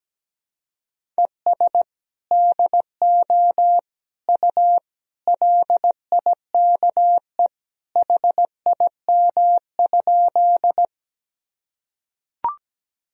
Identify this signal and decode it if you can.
Morse code, heard as a keyed tone, about 17 wpm, 700 Hz. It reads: ES DO U LIKE HIM?
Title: Sentences from Top 100 Words and Abbreviations 17wpm